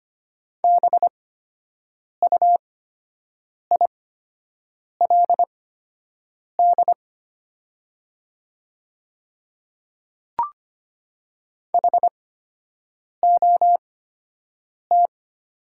Morse code, heard as a keyed tone, about 25 wpm, spaced as for 7 wpm, 700 Hz. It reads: BUILD HOT
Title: 8x Character Spacing - Top 200 Words - 25wpm